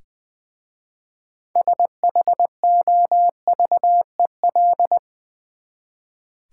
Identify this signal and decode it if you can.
Morse code, heard as a keyed tone, about 20 wpm, 700 Hz. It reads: SHOVEL
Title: Easy Words with Hints - 20wpm